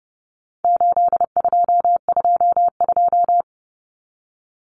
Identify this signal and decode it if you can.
Morse code, heard as a keyed tone, about 30 wpm, 700 Hz. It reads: 8222
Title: Common Cut Numbers - 4-Digit 30wpm